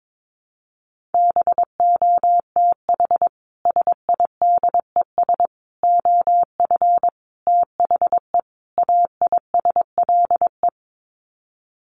Morse code, heard as a keyed tone, about 22 wpm, 700 Hz. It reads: BOTH SIDES OF THE AISLE
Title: Sets of 5 Words Encore2 22wpm